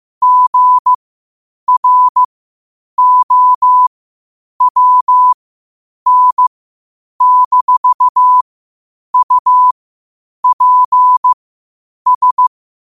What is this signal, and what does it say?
Morse code, heard as a keyed tone, about 15 wpm, spaced as for 8 wpm, 1000 Hz. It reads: GROWN-UPS
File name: qotdfiles / 8wpm